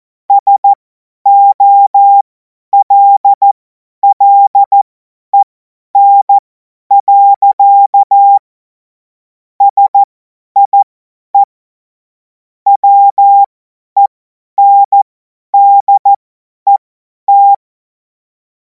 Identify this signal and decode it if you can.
Morse code, heard as a keyed tone, about 14 wpm, spaced as for 10 wpm, 800 Hz. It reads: SOLLEN. SIE WENDET
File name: Morsekurs / Witze / Witz 24. Februar 2026 50 F70